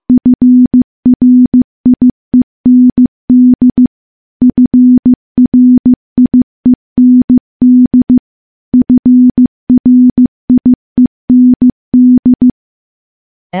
Morse code, heard as a keyed tone, about 15 wpm, 250 Hz. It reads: FRIEND FRIEND FRIEND
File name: morse-training-15WPM